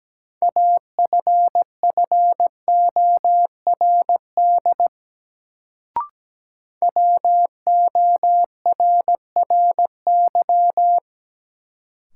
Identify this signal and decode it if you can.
Morse code, heard as a keyed tone, about 17 wpm, 700 Hz. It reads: AFFORD WORRY
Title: Sentences from Top 1100 Words - New Word Drill - 17wpm